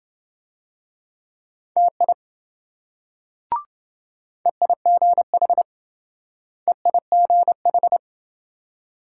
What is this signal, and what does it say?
Morse code, heard as a keyed tone, about 30 wpm, 700 Hz. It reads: TI EIGH EIGH